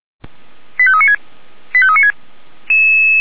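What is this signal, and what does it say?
Five-tone selective call (ZVEI), heard as quick runs of five tones, tones of about 70 ms. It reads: 85396 85396
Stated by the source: ZVEI five-tone selective call received by SDR.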